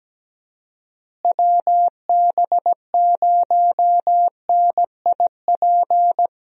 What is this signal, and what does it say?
Morse code, heard as a keyed tone, about 17 wpm, 700 Hz. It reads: WB0NIP
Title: USA Call Signs - 6 Characters - Repeated 2x - 17wpm